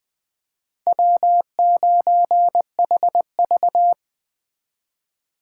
Morse code, heard as a keyed tone, about 24 wpm, 700 Hz. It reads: W9HV
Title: Speed Racing - USA 1x2 Call Signs 20wpm